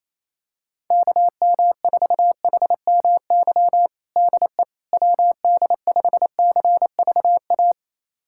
Morse code, heard as a keyed tone, about 28 wpm, 700 Hz. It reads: KM4HMY DE WD5CVA
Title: USA Callsign DE Callsign - 6 Characters - 28wpm